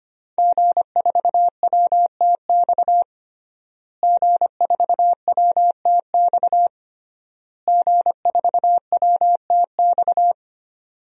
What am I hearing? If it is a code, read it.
Morse code, heard as a keyed tone, about 25 wpm, 700 Hz. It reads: G4WTX G4WTX G4WTX
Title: DXCC-100 Call Signs - 5 Characters - Repeated 3x - 25wpm